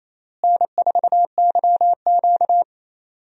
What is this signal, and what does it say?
Morse code, heard as a keyed tone, about 28 wpm, 700 Hz. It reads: N4YQ
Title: USA Call Signs - 4 Characters - Repeated 2x - 28wpm